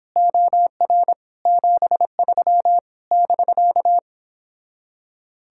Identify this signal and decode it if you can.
Morse code, heard as a keyed tone, about 29 wpm, 700 Hz. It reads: OR 73 <BK>